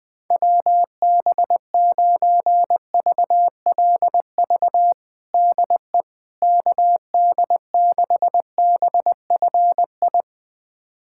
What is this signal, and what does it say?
Morse code, heard as a keyed tone, about 21 wpm, 700 Hz. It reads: WB9VLV DE KD6BFI
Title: USA Callsign DE Callsign 20wpm